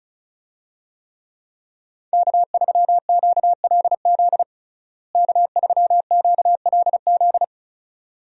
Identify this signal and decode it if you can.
Morse code, heard as a keyed tone, about 36 wpm, 700 Hz. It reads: K3QLZ K3QLZ
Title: USA Call Signs - 5 Characters - Repeated 2x - 35wpm